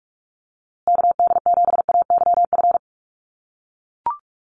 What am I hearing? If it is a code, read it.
Morse code, heard as a keyed tone, about 44 wpm, 700 Hz. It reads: KD7AYF
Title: USA Call Signs Repeated 3x 45wpm